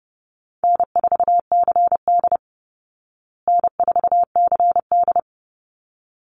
Morse code, heard as a keyed tone, about 30 wpm, 700 Hz. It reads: N4CD N4CD
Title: Easy Call Signs Repeated 2x - 30wpm